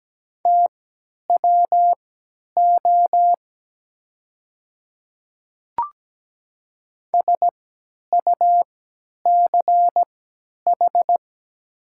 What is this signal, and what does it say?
Morse code, heard as a keyed tone, about 17 wpm, spaced as for 10 wpm, 700 Hz. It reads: TWO SUCH
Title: 3x Character Spacing - Top 200 Words - 17wpm